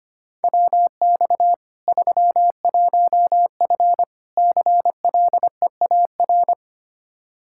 Morse code, heard as a keyed tone, about 25 wpm, 700 Hz. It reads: WX 31F CLEAR
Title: Practice QSO - Exchange 2 - Info - 25wpm